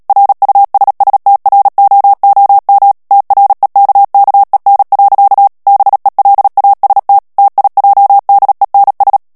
Morse code, heard as a keyed tone, about 37 wpm, 800 Hz. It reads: RUSSTROOM TREKKEN. BELAST TIJDENS
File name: morse mp3 / HOLLANDAIS texte clair 2 vitesse 35 mots minute